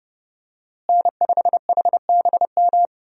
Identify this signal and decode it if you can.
Morse code, heard as a keyed tone, about 30 wpm, 700 Hz. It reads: N5HBM
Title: USA Call Signs - 5 Characters - Repeated 3x - 30wpm